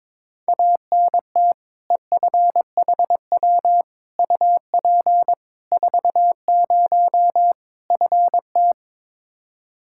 Morse code, heard as a keyed tone, about 22 wpm, 700 Hz. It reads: ANT EFHW UP 40 FT
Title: Practice QSO - Exchange 2 - Info - 22wpm